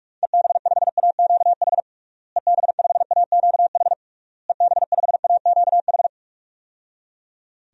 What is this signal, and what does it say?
Morse code, heard as a keyed tone, about 46 wpm, 700 Hz. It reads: EB5AQH EB5AQH EB5AQH
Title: DXCC-100 Call Signs - 6 Characters - Repeated 3x - 45wpm